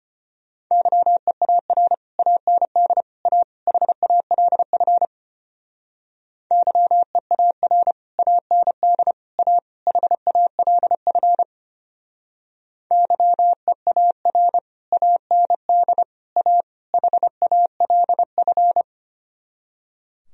Morse code, heard as a keyed tone, about 32 wpm, 700 Hz. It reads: YEAR AND A HALF YEAR AND A HALF YEAR AND A HALF